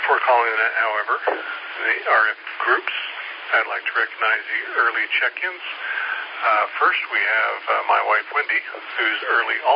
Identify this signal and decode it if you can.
Emergency Alert System (SAME) tones (header bursts), heard as harsh buzzing bursts, with speech. It is heard from 0.5 to 8.7 s.